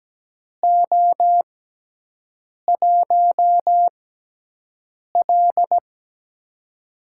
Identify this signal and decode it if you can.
Morse code, heard as a keyed tone, about 17 wpm, spaced as for 6 wpm, 700 Hz. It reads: O1L
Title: USA Call Signs - 6x Character Spacing - 17wpm